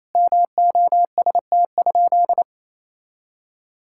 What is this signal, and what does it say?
Morse code, heard as a keyed tone, about 28 wpm, 700 Hz. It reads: MOST?
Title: Sentences from Top 1000 Words - Review Part 5 - 28wpm